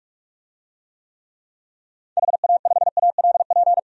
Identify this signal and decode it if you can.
Morse code, heard as a keyed tone, about 47 wpm, 700 Hz. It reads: HA5ALP
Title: DXCC-100 Call Signs - 6 Characters - 45wpm